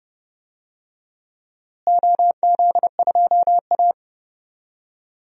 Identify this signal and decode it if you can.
Morse code, heard as a keyed tone, about 30 wpm, 700 Hz. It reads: OZ2A
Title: DXCC-100 Call Signs - 4 Characters - Repeated 3x - 30wpm